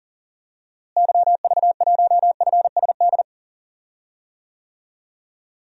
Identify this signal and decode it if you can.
Morse code, heard as a keyed tone, about 37 wpm, 700 Hz. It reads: YV1FSD
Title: DXCC-100 Call Signs - 6 Characters - Repeated 2x - 40wpm